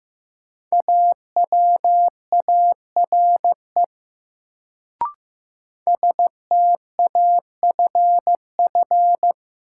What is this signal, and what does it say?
Morse code, heard as a keyed tone, about 15 wpm, 700 Hz. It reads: AWARE STAFF
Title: Sentences from Top 1200 Words - New Word Drill - 15wpm